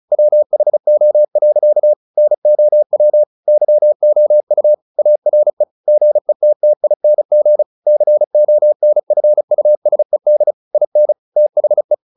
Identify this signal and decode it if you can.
Morse code, heard as a keyed tone, about 34 wpm, 600 Hz. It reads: WHO. NOW YOU ARE GETTING CONFUSED IN THE
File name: the old man and the sea 2 35wpm